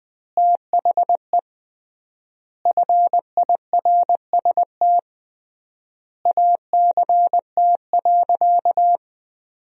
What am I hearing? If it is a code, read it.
Morse code, heard as a keyed tone, about 20 wpm, 700 Hz. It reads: THE FIRST ACT.
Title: 3x Word-Spacing - Sentences from Top 200 Words - 20wpm